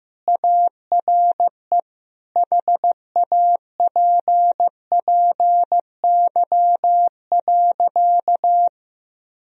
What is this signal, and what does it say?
Morse code, heard as a keyed tone, about 15 wpm, 700 Hz. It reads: ARE HAPPY.
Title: Sentences from Top 1000 Words - Review Part 3 - 15wpm